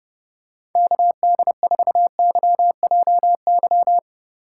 Morse code, heard as a keyed tone, about 30 wpm, 700 Hz. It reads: KD4YJY